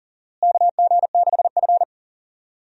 Morse code, heard as a keyed tone, about 40 wpm, 700 Hz. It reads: KG6F
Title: USA Call Signs - 4 Characters - Repeated 2x - 40wpm